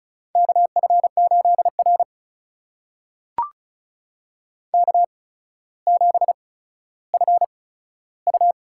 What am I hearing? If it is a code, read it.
Morse code, heard as a keyed tone, about 36 wpm, spaced as for 15 wpm, 700 Hz. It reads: KF8R K7FU
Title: USA Call Signs - 4 Characters - 8x Character Spacing - 35wpm